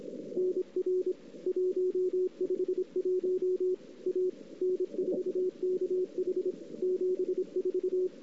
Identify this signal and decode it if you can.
Morse code, heard as a keyed tone, about 27 wpm, 410 Hz. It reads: NR 151 A N4KH 74
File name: CW / 2008-ss-cw-k5tr-op-n5rz -03:00